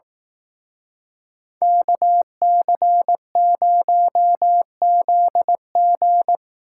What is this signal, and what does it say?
Morse code, heard as a keyed tone, about 18 wpm, 700 Hz. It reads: KC0ZG
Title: Speed Racing - USA 2x2 Call Signs 15wpm